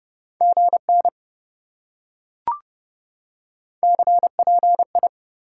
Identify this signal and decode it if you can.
Morse code, heard as a keyed tone, about 28 wpm, 700 Hz. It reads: GN CPI